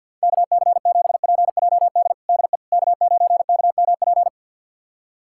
Morse code, heard as a keyed tone, about 50 wpm, 700 Hz. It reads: KC7PJD DE K9KMP
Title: USA Callsign DE Callsign 50wpm